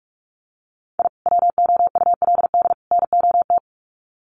Morse code, heard as a keyed tone, about 44 wpm, 700 Hz. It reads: I WOULD NOT
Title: Sets of 3 Words 45wpm